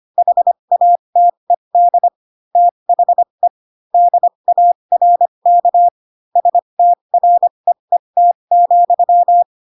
Morse code, heard as a keyed tone, about 25 wpm, spaced as for 22 wpm, 700 Hz. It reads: HATED THE DARK STREET,
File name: Pinocchio / Pinocchio ch6 18 25 ik1hge